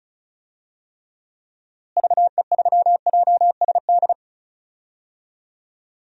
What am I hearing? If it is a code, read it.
Morse code, heard as a keyed tone, about 36 wpm, 700 Hz. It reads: VE3JSD
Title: DXCC-100 Call Signs - 6 Characters - Repeated 3x - 35wpm